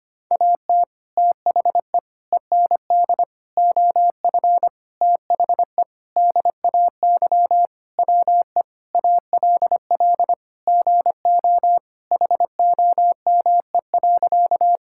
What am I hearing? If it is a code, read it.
Morse code, heard as a keyed tone, about 25 wpm, 700 Hz. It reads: AT THE END OF THE DAY WE ALL GO HOME.
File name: practice / 25wpm / Sentences with 4-Word Phrases - 25wpm